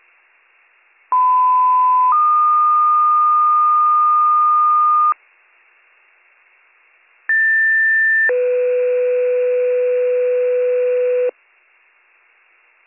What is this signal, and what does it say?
Two-tone page, heard as a one-second tone, then a three-second tone, about 1010 then 1220 Hz, about 1800 then 500 Hz. It is heard from 1.1 to 11.3 s.